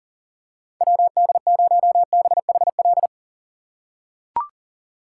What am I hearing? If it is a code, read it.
Morse code, heard as a keyed tone, about 40 wpm, 700 Hz. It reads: WD0BHL
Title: USA Call Signs - 6 Characters - 3x Character Spacing - 40wpm